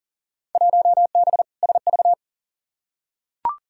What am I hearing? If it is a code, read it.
Morse code, heard as a keyed tone, about 40 wpm, 700 Hz. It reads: 1B SV
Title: Field Day Exchanges Encore - 40wpm